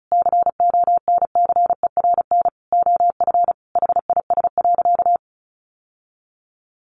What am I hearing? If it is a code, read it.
Morse code, heard as a keyed tone, about 34 wpm, 700 Hz. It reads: CONCERN OF HIS.
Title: Sentences from Top 1000 Words 35wpm